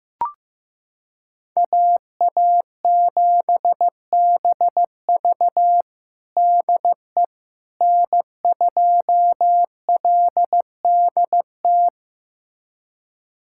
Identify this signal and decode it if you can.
Morse code, heard as a keyed tone, about 16 wpm, 700 Hz. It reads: AA7BV DE N2LDT
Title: USA Callsign DE Callsign - 5 Characters - 15wpm